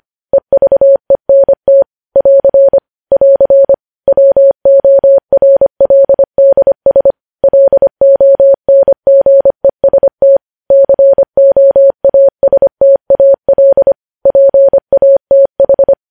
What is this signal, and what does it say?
Morse code, heard as a keyed tone, about 25 wpm, 550 Hz. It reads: EVENT <AR> <AR> WORLDS LONGEST COASTAL PATH
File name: podcast / News headlines-1x-25x25